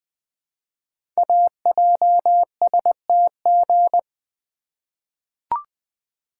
Morse code, heard as a keyed tone, about 21 wpm, 700 Hz. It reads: AJSTG
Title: Code Groups - 5 Letters - 20wpm